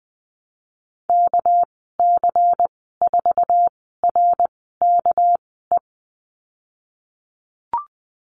Morse code, heard as a keyed tone, about 20 wpm, spaced as for 15 wpm, 700 Hz. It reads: KC4RKE
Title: 207 Lesson 33g - Letter j - Call Signs 20wpm